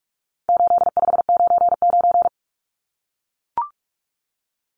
Morse code, heard as a keyed tone, about 44 wpm, 700 Hz. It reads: 8599
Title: Numbers - 4-Digit 45wpm